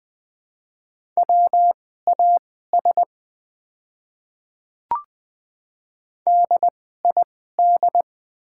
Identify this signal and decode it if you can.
Morse code, heard as a keyed tone, about 20 wpm, spaced as for 14 wpm, 700 Hz. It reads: WAS DID
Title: 2x Character Spacing - Top 200 Words - 20wpm